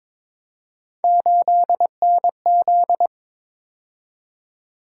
Morse code, heard as a keyed tone, about 27 wpm, 700 Hz. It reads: 8NZ
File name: Speed-Racing - Three Letter-Number Combinations - 22wpm